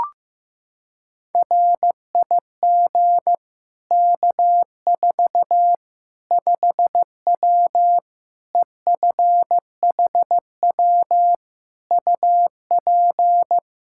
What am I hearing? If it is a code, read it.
Morse code, heard as a keyed tone, about 16 wpm, 700 Hz. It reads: RIG K4 5W EFHW UP
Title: Practice QSO - Exchange 2 - Info - 15wpm